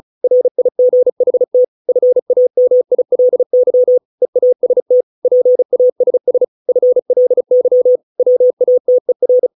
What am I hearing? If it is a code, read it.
Morse code, heard as a keyed tone, about 35 wpm, 500 Hz. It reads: RIGHT FAMILY EAST PASS FLY WATER
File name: 632 words / 632 Words 35WPM